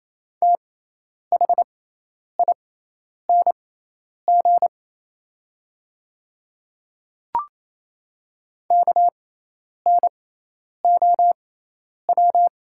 Morse code, heard as a keyed tone, about 28 wpm, spaced as for 10 wpm, 700 Hz. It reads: THING KNOW
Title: 6x Character Spacing - Top 100 Words - 28wpm